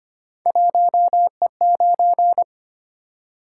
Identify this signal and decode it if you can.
Morse code, heard as a keyed tone, about 25 wpm, 700 Hz. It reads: 1E9
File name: Three Letter-Number Combinations - 25wpm